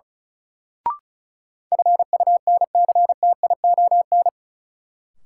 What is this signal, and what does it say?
Morse code, heard as a keyed tone, about 34 wpm, 700 Hz. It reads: FUNCTION
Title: Sentences from Top 1100 Words - New Word Drill Alt - 35wpm